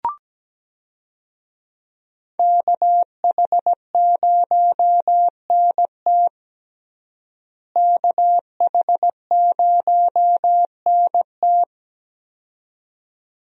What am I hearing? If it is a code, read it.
Morse code, heard as a keyed tone, about 17 wpm, 700 Hz. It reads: KH0NT KH0NT
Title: USA Call Signs - 5 Characters - Repeated 2x - 17wpm